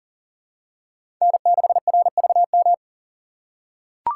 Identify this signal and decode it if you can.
Morse code, heard as a keyed tone, about 40 wpm, 700 Hz. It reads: N6RVM
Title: USA Call Signs - 5 Characters - 4x Character Spacing - 40wpm